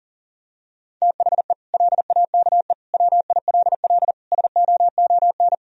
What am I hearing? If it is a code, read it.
Morse code, heard as a keyed tone, about 40 wpm, 700 Hz. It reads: THE LAKE WILL SOON
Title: Sentences from Top 1000 Words - Review Part 3 - 40wpm